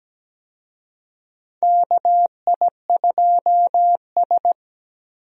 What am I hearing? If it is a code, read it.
Morse code, heard as a keyed tone, about 17 wpm, 700 Hz. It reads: KI2S
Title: USA Call Signs - 4 Characters - Repeated 2x - 17wpm